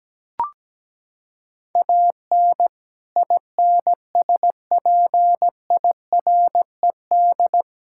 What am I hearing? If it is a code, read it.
Morse code, heard as a keyed tone, about 17 wpm, 700 Hz. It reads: AN INSPIRED
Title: Pangram Challenge v2 - 17wpm